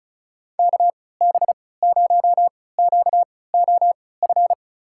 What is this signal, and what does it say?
Morse code, heard as a keyed tone, about 36 wpm, spaced as for 20 wpm, 700 Hz. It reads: KB0QOF
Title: USA Call Signs - 6 Characters - 3x Character Spacing - 35wpm